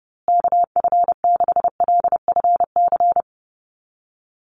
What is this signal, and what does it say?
Morse code, heard as a keyed tone, about 30 wpm, 700 Hz. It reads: KF6LFC
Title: USA Call Signs Repeated 3x 30wpm